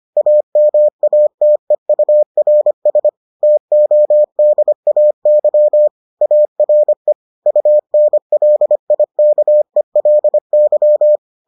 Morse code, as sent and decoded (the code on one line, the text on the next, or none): .- -- .- - . ..- .-. ... / - --- -.. .- -.-- / .- .-. / ..- -. .-.. .. -.- . .-.. -.--
AMATEURS TODAY ARE UNLIKELY